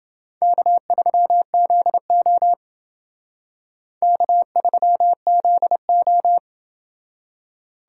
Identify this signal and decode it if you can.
Morse code, heard as a keyed tone, about 28 wpm, 700 Hz. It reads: K3ZO K3ZO